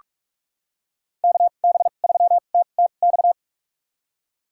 Morse code, heard as a keyed tone, about 46 wpm, spaced as for 32 wpm, 700 Hz. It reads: KB3TTX
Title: USA Call Signs - 6 Characters - 2x Character Spacing - 45wpm